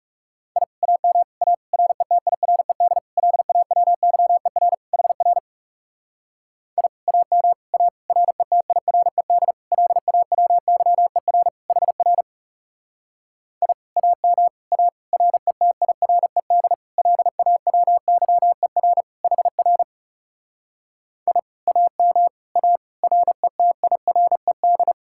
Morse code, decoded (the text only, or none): I AM A RETIRED LAWYER HR I AM A RETIRED LAWYER HR I AM A RETIRED LAWYER HR I AM A RETIRED